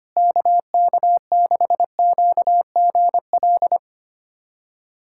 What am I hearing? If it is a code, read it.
Morse code, heard as a keyed tone, about 25 wpm, 700 Hz. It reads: KK6QGL
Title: USA Callsign DE Callsign 25wpm